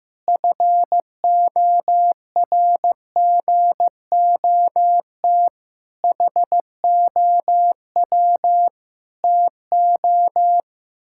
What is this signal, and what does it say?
Morse code, heard as a keyed tone, about 15 wpm, 700 Hz. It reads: FORGOT HOW TO
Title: Pangram Challenge v2 - 15wpm